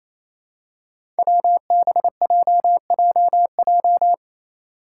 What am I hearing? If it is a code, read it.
Morse code, heard as a keyed tone, about 27 wpm, 700 Hz. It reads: WBJJJ
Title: Five Challenging Letter-Number Combinations 28wpm